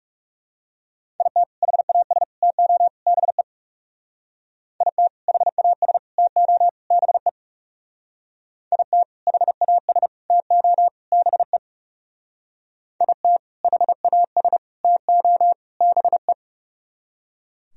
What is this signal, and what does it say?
Morse code, heard as a keyed tone, about 38 wpm, 700 Hz. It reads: IT HAS TO BE IT HAS TO BE IT HAS TO BE IT HAS TO BE